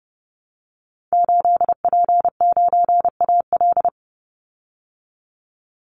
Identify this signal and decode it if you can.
Morse code, heard as a keyed tone, about 30 wpm, 700 Hz. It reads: 8P9AL
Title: Call Signs Repeated 2x - 30wpm